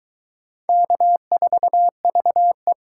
Morse code, heard as a keyed tone, about 22 wpm, 700 Hz. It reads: K4VE